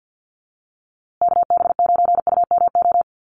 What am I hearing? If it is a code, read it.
Morse code, heard as a keyed tone, about 50 wpm, 700 Hz. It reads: KB9UMO
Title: USA Call Signs Repeated 2x 50wpm